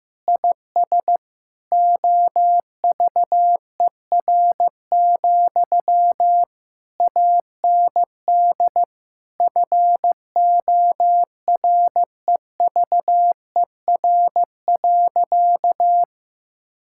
Morse code, heard as a keyed tone, about 16 wpm, 700 Hz. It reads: IS OVER, AND FOREVER.
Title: Sentences from Top 1600 Words 15wpm